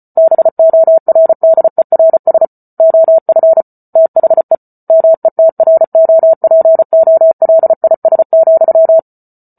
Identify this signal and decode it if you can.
Morse code, heard as a keyed tone, about 34 wpm, 650 Hz. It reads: BORDERS OF THE METROPOLIS,